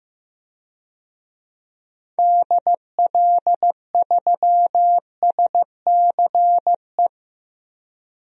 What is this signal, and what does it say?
Morse code, heard as a keyed tone, about 15 wpm, 700 Hz. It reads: DL3SCE
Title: DXCC-100 Call Signs - 6 Characters - Repeated 3x - 15wpm